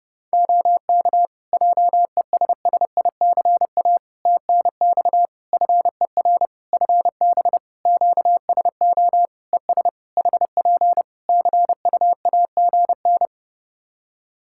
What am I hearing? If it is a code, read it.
Morse code, heard as a keyed tone, about 29 wpm, 700 Hz. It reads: OK JESSICA TNX FER FB QSO ES HP CUAGN
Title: Practice QSO - Exchange 4 - Ending - 30wpm